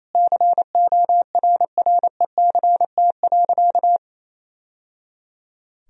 Morse code, heard as a keyed tone, about 28 wpm, 700 Hz. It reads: CORRECT.
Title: Sentences from Top 1000 Words - Review Part 2 - 28wpm